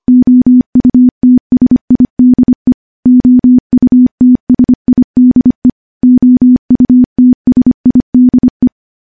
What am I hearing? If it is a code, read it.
Morse code, heard as a keyed tone, about 25 wpm, 250 Hz. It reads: OUTSIDE OUTSIDE OUTSIDE
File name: morse-training-25WPM